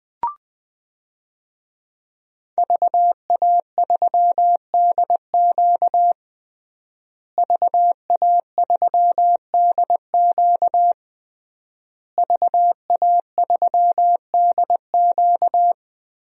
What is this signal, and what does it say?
Morse code, heard as a keyed tone, about 20 wpm, 700 Hz. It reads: VA3DQ VA3DQ VA3DQ
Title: DXCC-100 Call Signs - 5 Characters - Repeated 3x - 20wpm